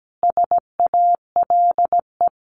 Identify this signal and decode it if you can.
Morse code, heard as a keyed tone, about 17 wpm, 700 Hz. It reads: SALE
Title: Sentences from Top 1400 Words 17wpm